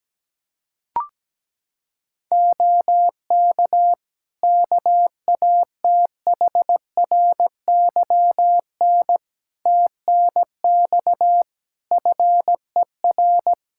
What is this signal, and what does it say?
Morse code, heard as a keyed tone, about 17 wpm, 700 Hz. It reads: OK KATHRYN TNX FER
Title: Practice QSO - Exchange 4 - Ending - 17wpm